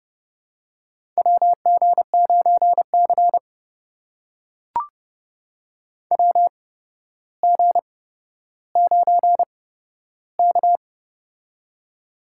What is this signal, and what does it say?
Morse code, heard as a keyed tone, about 30 wpm, spaced as for 13 wpm, 700 Hz. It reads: WG9C WG9K
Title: USA Call Signs - 4 Characters - 8x Character Spacing - 30wpm